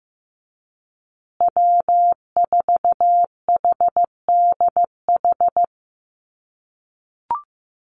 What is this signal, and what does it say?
Morse code, heard as a keyed tone, about 15 wpm, 700 Hz. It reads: W4HDH
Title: USA Call Signs Repeated 2x 15wpm